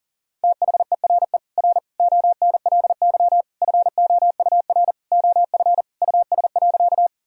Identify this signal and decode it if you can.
Morse code, heard as a keyed tone, about 40 wpm, 700 Hz. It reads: THERE R ONLY FOUR OF US.
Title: Sentences from Top 300 Words and Abbreviations 40wpm